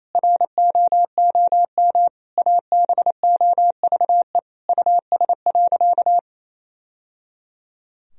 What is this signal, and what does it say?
Morse code, heard as a keyed tone, about 28 wpm, 700 Hz. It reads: ROOM ABOVE US.
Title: Sentences from Top 1000 Words - Review Part 3 - 28wpm